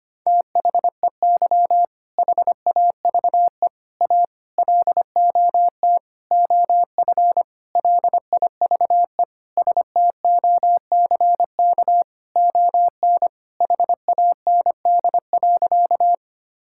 Morse code, heard as a keyed tone, about 25 wpm, 700 Hz. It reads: THEY HAVE A LOT OF LIVE STOCK ON HAND.
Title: Sentences from Top 1000 Words - Review Part 5 - 25wpm